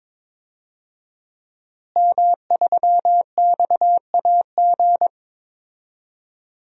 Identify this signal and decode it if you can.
Morse code, heard as a keyed tone, about 22 wpm, 700 Hz. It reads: M3XAG
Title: DXCC-100 Call Signs - 5 Characters - Repeated 3x - 22wpm